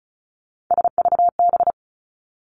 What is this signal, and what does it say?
Morse code, heard as a keyed tone, about 36 wpm, 700 Hz. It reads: SVB
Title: Three Challenging Letter-Number Combinations 35wpm